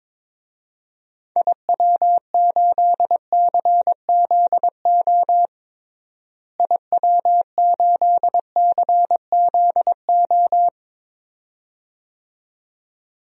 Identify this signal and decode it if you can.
Morse code, heard as a keyed tone, about 22 wpm, 700 Hz. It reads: IW8CZO IW8CZO
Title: DXCC-100 Call Signs - 6 Characters - Repeated 2x - 22wpm